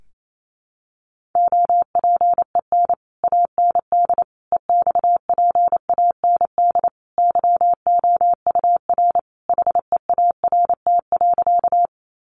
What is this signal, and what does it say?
Morse code, heard as a keyed tone, about 28 wpm, 700 Hz. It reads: OPEN AND EXPAND YOUR HEART.
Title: Sentences from Top 1400 Words 28wpm